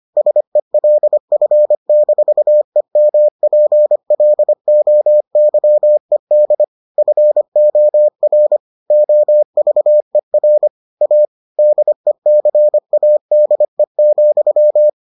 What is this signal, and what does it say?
Morse code, heard as a keyed tone, about 25 wpm, 600 Hz. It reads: SELF-EMPLOYED FOR OVER A DECADE,